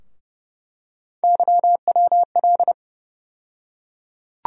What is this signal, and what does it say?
Morse code, heard as a keyed tone, about 30 wpm, 700 Hz. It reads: YWL
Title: USA Call Sign Postfixes - 3 Letters 30wpm